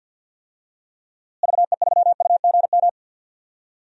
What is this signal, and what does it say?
Morse code, heard as a keyed tone, about 52 wpm, 700 Hz. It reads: VE3UGM
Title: DXCC-100 Call Signs - 6 Characters - Repeated 2x - 50wpm